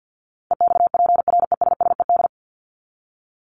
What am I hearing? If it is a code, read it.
Morse code, heard as a keyed tone, about 49 wpm, 700 Hz. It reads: EXPRESSED
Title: ED Suffix - Top 100 Words with Base Words 50wpm